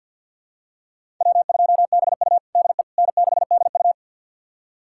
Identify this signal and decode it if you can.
Morse code, heard as a keyed tone, about 52 wpm, 700 Hz. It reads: W2BU DE N6DU
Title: USA Callsign DE Callsign - 4 Characters - 50wpm